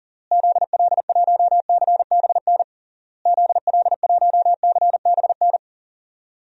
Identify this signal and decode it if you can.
Morse code, heard as a keyed tone, about 40 wpm, 700 Hz. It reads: ZL1CBN ZL1CBN
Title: DXCC-100 Call Signs - 6 Characters - Repeated 3x - 40wpm